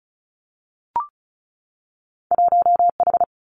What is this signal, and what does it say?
Morse code, heard as a keyed tone, about 36 wpm, 700 Hz. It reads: 1H
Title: Winter Field Day Exchanges 35wpm